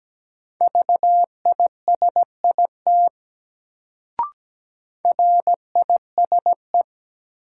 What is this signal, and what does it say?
Morse code, heard as a keyed tone, about 17 wpm, 700 Hz. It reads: VISIT RISE
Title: Sentences from Top 900 Words - New Word Drill - 17wpm